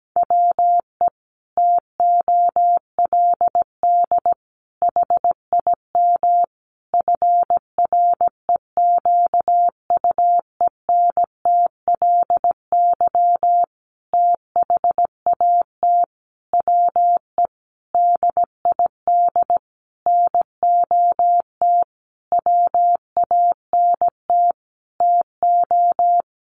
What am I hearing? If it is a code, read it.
Morse code, heard as a keyed tone, about 17 wpm, 700 Hz. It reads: WE TOLD HIM FREQUENTLY THAT WE DID NOT WANT TO
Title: Sentences from Top 1300 Words 17wpm